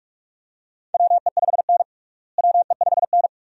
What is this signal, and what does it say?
Morse code, heard as a keyed tone, about 46 wpm, 700 Hz. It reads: WE5N WE5N
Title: USA Call Signs - 4 Characters - Repeated 2x - 45wpm